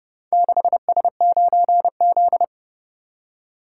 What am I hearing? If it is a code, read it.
Morse code, heard as a keyed tone, about 29 wpm, 700 Hz. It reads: 6S9Z